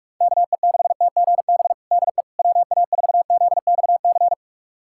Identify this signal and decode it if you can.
Morse code, heard as a keyed tone, about 46 wpm, 700 Hz. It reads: KE6TGB DE WA4ZXC
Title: USA Callsign DE Callsign 45wpm